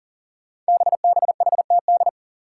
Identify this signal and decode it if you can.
Morse code, heard as a keyed tone, about 37 wpm, 700 Hz. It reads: BBHTD
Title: Five Challenging Letter-Number Combinations 40wpm